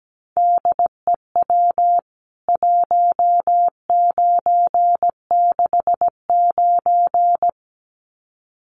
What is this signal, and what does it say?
Morse code, heard as a keyed tone, about 17 wpm, 700 Hz. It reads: DEW 1969
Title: CWT Exchange 17wpm